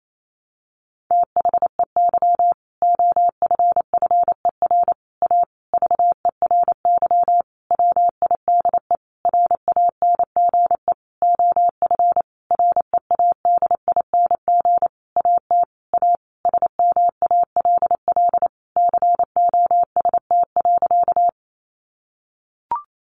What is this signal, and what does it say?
Morse code, heard as a keyed tone, about 28 wpm, 690 Hz. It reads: THEY OFFER A VERY WIDE RANGE OF READING AT A SMALL COST.
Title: Sentences from Top 900 Words 28wpm